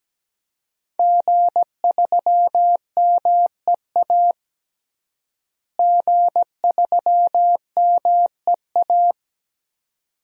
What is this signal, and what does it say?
Morse code, heard as a keyed tone, about 17 wpm, 700 Hz. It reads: G3MEA G3MEA
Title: DXCC-100 Call Signs - 5 Characters - Repeated 2x - 17wpm